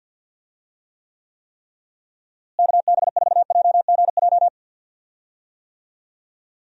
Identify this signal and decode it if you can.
Morse code, heard as a keyed tone, about 51 wpm, 700 Hz. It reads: KB4JGJ
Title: USA Call Signs - 6 Characters - Repeated 3x - 50wpm